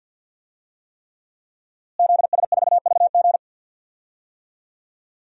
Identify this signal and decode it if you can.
Morse code, heard as a keyed tone, about 52 wpm, 700 Hz. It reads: ZS4VG
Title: DXCC-100 Call Signs - 5 Characters - Repeated 3x - 50wpm